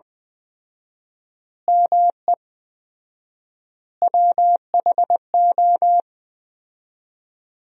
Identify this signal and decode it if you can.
Morse code, heard as a keyed tone, about 21 wpm, 700 Hz. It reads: ME WHO
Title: 4x Word-Spacing - Sentences from Top 200 Words - 20wpm